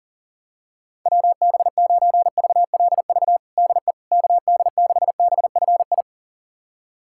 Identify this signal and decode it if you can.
Morse code, heard as a keyed tone, about 40 wpm, 700 Hz. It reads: WB9VLV DE KD6BFI
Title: USA Callsign DE Callsign 40wpm